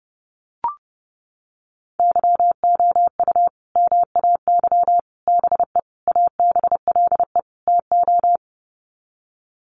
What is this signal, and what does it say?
Morse code, heard as a keyed tone, about 28 wpm, 700 Hz. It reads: YOU MAY BE ABLE TO
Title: Sets of 5 Words Encore 30wpm